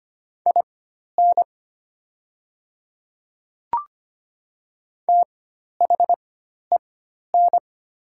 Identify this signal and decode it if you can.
Morse code, heard as a keyed tone, about 25 wpm, spaced as for 12 wpm, 700 Hz. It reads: IN THEN
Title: 4x Character Spacing - Top 100 Words - 25wpm